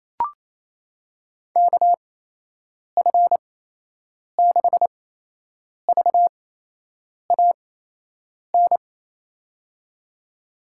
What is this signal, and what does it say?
Morse code, heard as a keyed tone, about 27 wpm, spaced as for 8 wpm, 700 Hz. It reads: KF6VAN
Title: USA Call Signs - 6 Characters - 8x Character Spacing - 28wpm